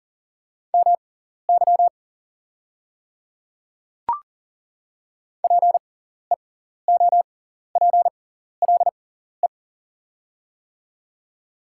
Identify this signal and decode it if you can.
Morse code, heard as a keyed tone, about 40 wpm, spaced as for 14 wpm, 700 Hz. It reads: MY PEOPLE
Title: 6x Character Spacing - Top 100 Words - 40wpm